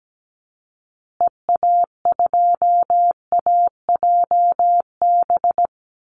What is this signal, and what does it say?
Morse code, heard as a keyed tone, about 17 wpm, 700 Hz. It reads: EA2AJB
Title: Call Signs Repeated 2x - 17wpm